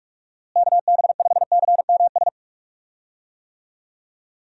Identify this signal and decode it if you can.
Morse code, heard as a keyed tone, about 46 wpm, 700 Hz. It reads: KB5CMS
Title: USA Call Signs - 6 Characters - Repeated 3x - 45wpm